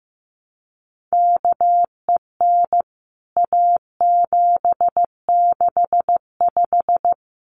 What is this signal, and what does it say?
Morse code, heard as a keyed tone, about 15 wpm, 700 Hz. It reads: KEN A765
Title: CWT Exchange 15wpm